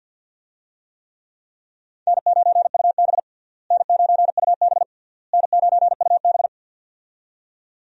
Morse code, heard as a keyed tone, about 48 wpm, 700 Hz. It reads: N9UB N9UB N9UB
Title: USA Call Signs - 4 Characters - Repeated 3x - 50wpm